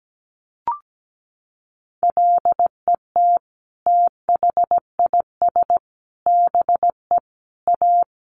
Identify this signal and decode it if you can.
Morse code, heard as a keyed tone, about 17 wpm, 700 Hz. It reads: LET THIS BE A
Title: Sentences from Top 1200 Words 17wpm